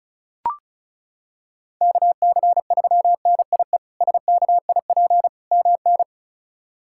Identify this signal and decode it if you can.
Morse code, heard as a keyed tone, about 34 wpm, 700 Hz. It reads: KC3NIE SKIP MN
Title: K1USN SST QSO Practice - 35wpm